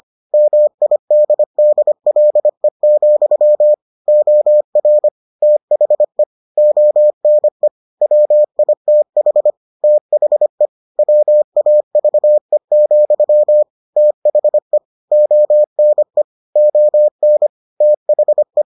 Morse code, heard as a keyed tone, about 25 wpm, 600 Hz. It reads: MIDDLE, OR THE ONE WITH THE WAVE, THE ONE ON THE